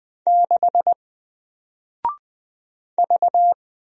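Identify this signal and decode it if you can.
Morse code, heard as a keyed tone, about 20 wpm, 700 Hz. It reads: 6 V
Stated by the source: Single letter and number Morse drill at 20 wpm, from N4LCD.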